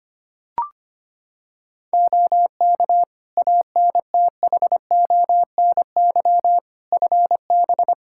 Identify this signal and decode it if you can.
Morse code, heard as a keyed tone, about 25 wpm, 700 Hz. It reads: OK ANTHONY FB
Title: Practice QSO - Exchange 2 - Info - 25wpm